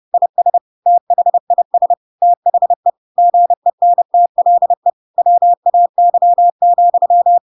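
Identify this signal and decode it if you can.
Morse code, heard as a keyed tone, about 30 wpm, 700 Hz. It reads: IS THIS THE GENTLE WAY,
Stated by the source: Chapter 2 of The Adventures of Pinocchio read as a book in Morse code (CW) at 30 wpm, by ik1hge.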